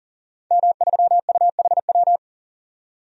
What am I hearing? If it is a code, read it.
Morse code, heard as a keyed tone, about 37 wpm, 700 Hz. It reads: M3UHW
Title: DXCC-100 Call Signs - 5 Characters - 40wpm